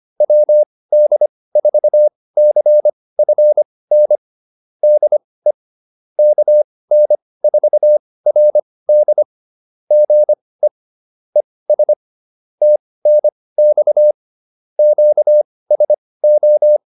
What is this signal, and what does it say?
Morse code, heard as a keyed tone, about 25 wpm, spaced as for 18 wpm, 600 Hz. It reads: WD4CFN DE KN4RD GE ES TNX QSO
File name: QSO207 18